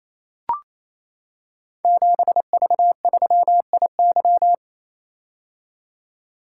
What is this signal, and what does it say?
Morse code, heard as a keyed tone, about 28 wpm, 700 Hz. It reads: 7V3IY
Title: Code Groups - 5 Letter:Numbers - 28wpm